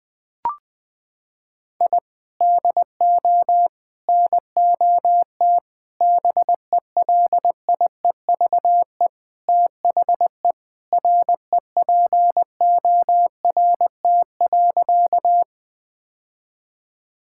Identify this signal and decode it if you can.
Morse code, heard as a keyed tone, about 20 wpm, 700 Hz. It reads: I DO NOT BELIEVE THE REPORT.
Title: Sentences from Top 1000 Words - Review Part 6 - 20wpm